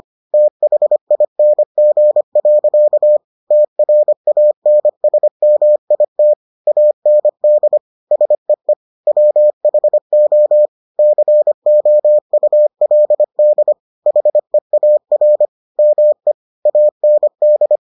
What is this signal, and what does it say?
Morse code, heard as a keyed tone, about 25 wpm, 600 Hz. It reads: THING. TRANSMIT AND SEE WHO COULD HEAR ME AND